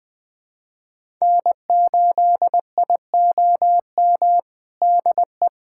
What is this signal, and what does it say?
Morse code, heard as a keyed tone, about 20 wpm, 700 Hz. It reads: N8IOM DE
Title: USA Callsign DE Callsign - 5 Characters - 20wpm